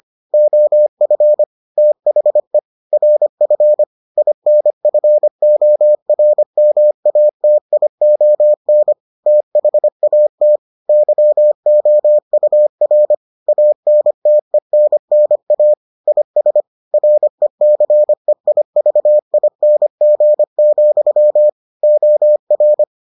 Morse code, heard as a keyed tone, about 25 wpm, 600 Hz. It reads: OF THE RF INFORMATION THAT YOUR ANTENNA IS RECEIVING, OR